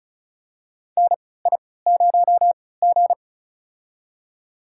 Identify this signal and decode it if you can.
Morse code, heard as a keyed tone, about 36 wpm, spaced as for 20 wpm, 700 Hz. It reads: NI0G
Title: USA Call Signs - 4 Characters - 3x Character Spacing - 35wpm